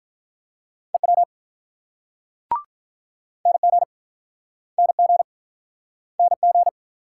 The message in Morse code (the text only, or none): EW NG NG NG